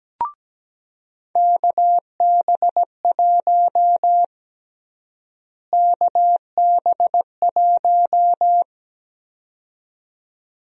Morse code, heard as a keyed tone, about 17 wpm, 700 Hz. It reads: KB1 KB1
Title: USA Call Sign Prefixes - Repeated 2x - 17wpm